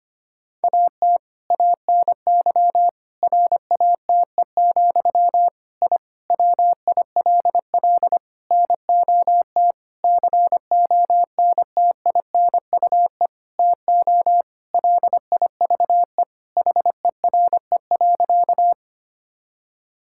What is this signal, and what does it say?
Morse code, heard as a keyed tone, about 25 wpm, 700 Hz. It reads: AT ANY RATE, I WILL NOT CONTINUE TO LIVE HERE.
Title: Sentences from Top 1000 Words - Review Part 6 - 25wpm